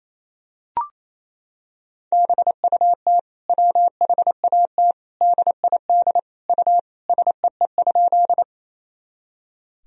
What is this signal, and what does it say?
Morse code, heard as a keyed tone, about 28 wpm, 700 Hz. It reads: BUT WHAT DID U SEE?
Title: Sentences from Top 100 Words and Abbreviations 28wpm